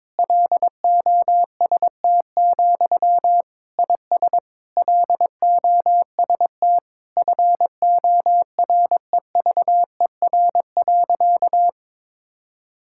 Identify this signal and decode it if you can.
Morse code, heard as a keyed tone, about 22 wpm, 700 Hz. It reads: LOST, IS LOST FOREVER.
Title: Sentences from Top 1600 Words 22wpm